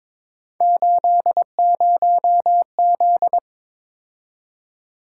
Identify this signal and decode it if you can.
Morse code, heard as a keyed tone, about 22 wpm, 700 Hz. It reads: 80Z
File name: practice / 22wpm / Three Letter-Number Combinations - 22wpm